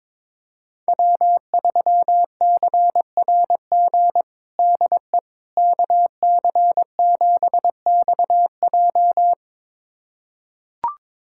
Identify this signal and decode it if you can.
Morse code, heard as a keyed tone, about 21 wpm, 700 Hz. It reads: W3CRG DE KC7XJ
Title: USA Callsign DE Callsign - 5 Characters - 22wpm